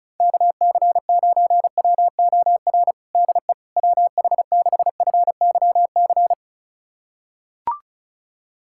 Morse code, heard as a keyed tone, about 36 wpm, 700 Hz. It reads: KC9WOR DE WH6FYC
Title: USA Callsign DE Callsign 35wpm